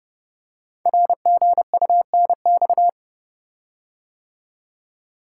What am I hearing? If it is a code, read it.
Morse code, heard as a keyed tone, about 28 wpm, 690 Hz. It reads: RGUNX
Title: Code Groups - 5 Letters - 30wpm